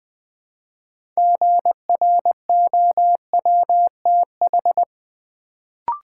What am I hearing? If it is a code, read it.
Morse code, heard as a keyed tone, about 20 wpm, 700 Hz. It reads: GROWTH
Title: Build A Word - 2 Letters At A Time - 6-Letter Words - 20wpm